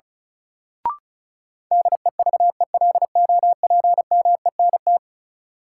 Morse code, heard as a keyed tone, about 34 wpm, 700 Hz. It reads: DEVELOPMENT
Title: Sentences from Top 1400 Words - New Word Drill Alt - 35wpm